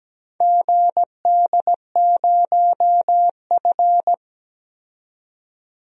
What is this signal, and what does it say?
Morse code, heard as a keyed tone, about 17 wpm, 700 Hz. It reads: GD0F
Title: DXCC-100 Call Signs - 4 Characters - 17wpm